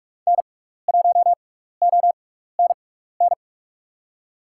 Morse code, heard as a keyed tone, about 44 wpm, spaced as for 15 wpm, 700 Hz. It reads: N1ONN
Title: USA Call Signs - 6x Character Spacing - 45wpm